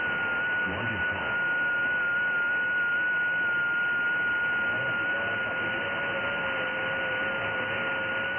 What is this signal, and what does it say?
FT8.
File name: IC-7100